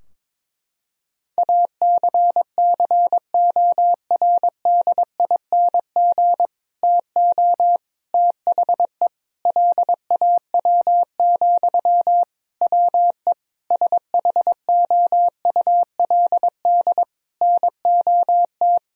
Morse code, heard as a keyed tone, about 22 wpm, 700 Hz. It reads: ACCORDING TO THE LAW, WE SHOULD NOT
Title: Sentences from Top 1000 Words - Review Part 4 - 22wpm